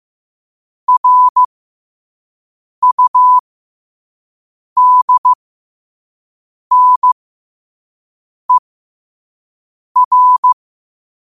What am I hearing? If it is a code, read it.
Morse code, heard as a keyed tone, about 15 wpm, spaced as for 5 wpm, 1000 Hz. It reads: RUDNER
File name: 5wpm